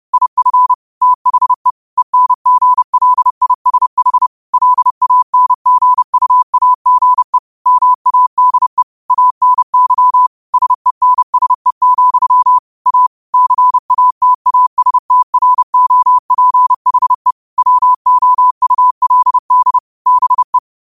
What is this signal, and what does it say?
Morse code, heard as a keyed tone, about 30 wpm, 1000 Hz. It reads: IF THE ENGLISH LANGUAGE MADE ANY SENSE, A CATASTROPHE WOULD BE